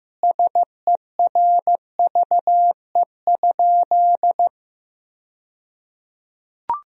Morse code, heard as a keyed tone, about 15 wpm, 700 Hz. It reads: SERVE?
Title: 2x Word-Spacing - Sentences from Top 400 Words - 15wpm